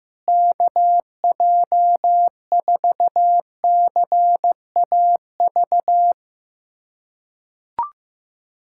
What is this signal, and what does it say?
Morse code, heard as a keyed tone, about 15 wpm, 700 Hz. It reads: KJ4CAV
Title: USA Callsign DE Callsign 15wpm